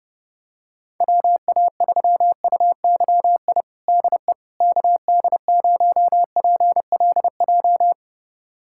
Morse code, heard as a keyed tone, about 30 wpm, 700 Hz. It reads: WA3UYI DE KD0PLJ
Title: USA Callsign DE Callsign - 6 Characters - 30wpm